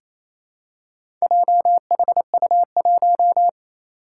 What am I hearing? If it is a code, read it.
Morse code, heard as a keyed tone, about 28 wpm, 700 Hz. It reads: JHU1
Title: Four Challenging Letter-Number Combinations 28wpm